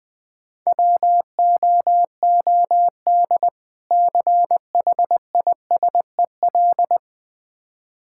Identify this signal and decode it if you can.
Morse code, heard as a keyed tone, about 20 wpm, 700 Hz. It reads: WOOD CHISEL
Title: Difficult Words - Speed-Racing with Hints - 20wpm